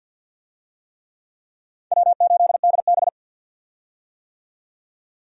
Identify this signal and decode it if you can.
Morse code, heard as a keyed tone, about 52 wpm, 700 Hz. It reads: W8DB
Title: USA Call Signs - 4 Characters - 50wpm